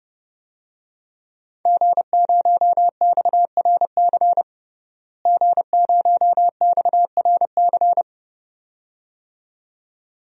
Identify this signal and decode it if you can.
Morse code, heard as a keyed tone, about 32 wpm, 700 Hz. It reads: G0XRC G0XRC
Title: DXCC-100 Call Signs - 5 Characters - Repeated 2x - 30wpm